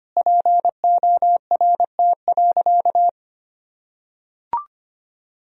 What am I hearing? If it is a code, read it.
Morse code, heard as a keyed tone, about 25 wpm, 700 Hz. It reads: PORT.
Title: Sentences from Top 1000 Words - Review Part 2 - 25wpm